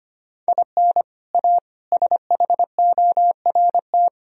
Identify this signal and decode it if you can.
Morse code, heard as a keyed tone, about 25 wpm, 700 Hz. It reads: IN A SHORT
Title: Sentences from Top 1000 Words - Review Part 2 - 25wpm